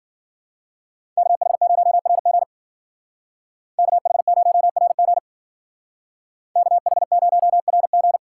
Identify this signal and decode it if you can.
Morse code, heard as a keyed tone, about 48 wpm, 700 Hz. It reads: KH0RG KH0RG KH0RG
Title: Speed Racing - USA 2x2 Call Signs 40wpm